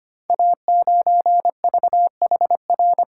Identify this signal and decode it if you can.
Morse code, heard as a keyed tone, about 25 wpm, 700 Hz. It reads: A9VHR
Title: Code Groups - 5 Letter:Numbers - 25wpm